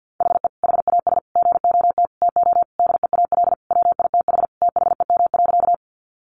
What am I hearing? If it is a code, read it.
Morse code, heard as a keyed tone, about 50 wpm, 700 Hz. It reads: HE HAS GOT TO DEAL WITH THEM.
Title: Sentences from Top 800 Words 50wpm